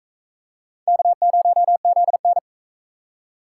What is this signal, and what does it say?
Morse code, heard as a keyed tone, about 40 wpm, 700 Hz. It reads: K0ZN